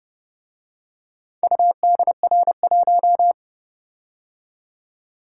Morse code, heard as a keyed tone, about 29 wpm, 700 Hz. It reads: UDR1